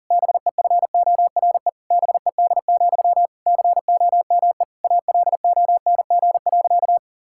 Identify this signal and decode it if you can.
Morse code, heard as a keyed tone, about 40 wpm, 700 Hz. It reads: BEFORE BED, COME ALONG.
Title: Sentences from Top 1600 Words 40wpm